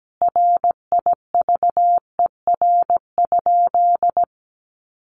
Morse code, heard as a keyed tone, about 17 wpm, 700 Hz. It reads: RIVER?
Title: Sentences from Top 300 Words 17wpm